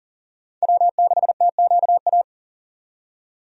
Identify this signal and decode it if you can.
Morse code, heard as a keyed tone, about 40 wpm, 700 Hz. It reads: W6TQA
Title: USA Call Signs - 5 Characters - 40wpm